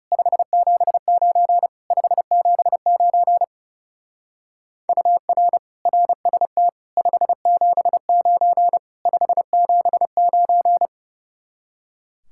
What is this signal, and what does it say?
Morse code, heard as a keyed tone, about 35 wpm, 700 Hz. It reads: 579 579 UR RST 579 579
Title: Speed-Racing Common QSO Phrases - 30wpm